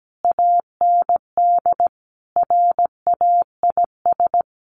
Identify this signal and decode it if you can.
Morse code, heard as a keyed tone, about 17 wpm, 700 Hz. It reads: AND RAIS
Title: A Sound Of Thunder 17wpm